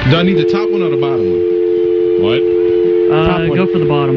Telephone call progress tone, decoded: dial tone